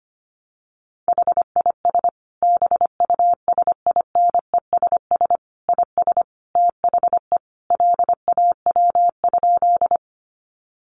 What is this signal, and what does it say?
Morse code, heard as a keyed tone, about 25 wpm, 700 Hz. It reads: HIS BUSINESS IS THE LAW?
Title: Sentences from Top 600 Words 25wpm